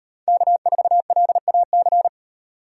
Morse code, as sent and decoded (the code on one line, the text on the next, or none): -.- ....- .-.. .- -.-.
K4LAC